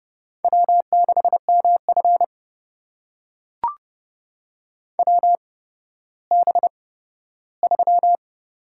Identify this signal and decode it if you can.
Morse code, heard as a keyed tone, about 30 wpm, spaced as for 15 wpm, 700 Hz. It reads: W6MF WB3
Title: USA Call Signs - 8x Character Spacing - 30wpm